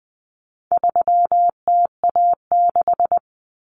Morse code, heard as a keyed tone, about 20 wpm, 700 Hz. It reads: 3TA6
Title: Common Cut Numbers - 4-Digit 20wpm